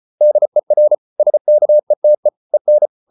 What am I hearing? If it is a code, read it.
Morse code, heard as a keyed tone, about 34 wpm, 600 Hz. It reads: DER SKETE EN
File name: Ferie QRQ 34wpm